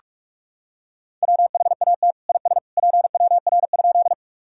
Morse code, heard as a keyed tone, about 46 wpm, 700 Hz. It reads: WHAT IS PWR?
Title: Sentences from Top 500 Words and Abbreviations 45wpm